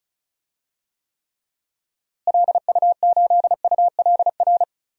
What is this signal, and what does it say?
Morse code, heard as a keyed tone, about 36 wpm, 700 Hz. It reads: LU8ULR
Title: DXCC-100 Call Signs - 6 Characters - Repeated 3x - 35wpm